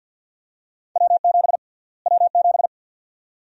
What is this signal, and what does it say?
Morse code, heard as a keyed tone, about 51 wpm, 700 Hz. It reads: W7 W7
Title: USA Call Sign Prefixes - Repeated 2x - 50wpm